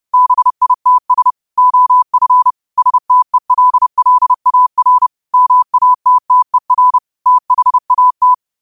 Morse code, heard as a keyed tone, about 30 wpm, 1000 Hz. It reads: BITS OF STELLAR MATTER THAT